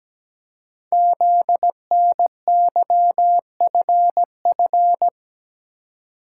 Morse code, heard as a keyed tone, about 17 wpm, 700 Hz. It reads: ZNYFF
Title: Code Groups - 5 Letters - 17wpm